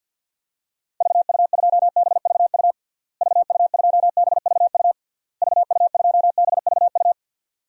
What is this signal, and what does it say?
Morse code, heard as a keyed tone, about 52 wpm, 700 Hz. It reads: VU2BVU VU2BVU VU2BVU
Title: DXCC-100 Call Signs - 6 Characters - Repeated 3x - 50wpm